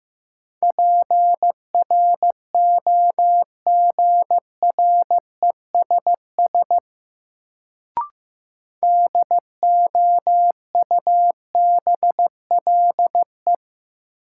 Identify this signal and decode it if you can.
Morse code, heard as a keyed tone, about 15 wpm, 700 Hz. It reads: PROGRESS DOUBLE
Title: Sentences from Top 1400 Words - New Word Drill Alt - 15wpm